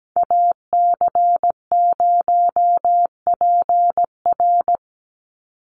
Morse code, heard as a keyed tone, about 17 wpm, 700 Hz. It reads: AC0PR
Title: NA SOTA Activator Callsigns 17wpm